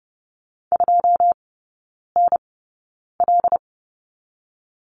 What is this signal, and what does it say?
Morse code, heard as a keyed tone, about 30 wpm, spaced as for 9 wpm, 700 Hz. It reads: 2NL
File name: 081 Lesson 14e - Number 2 - Three Characters 30wpm